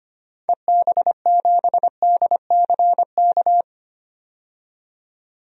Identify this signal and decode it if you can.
Morse code, heard as a keyed tone, about 25 wpm, 700 Hz. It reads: EB7DCK
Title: DXCC-100 Call Signs - 6 Characters - 25wpm